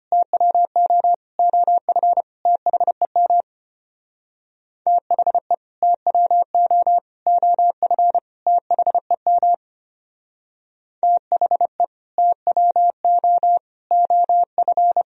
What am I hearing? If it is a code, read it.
Morse code, heard as a keyed tone, about 30 wpm, 700 Hz. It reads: TWO OF THEM THE TWO OF THEM THE TWO OF
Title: Speed-Racing - Sets of 4 Words 25wpm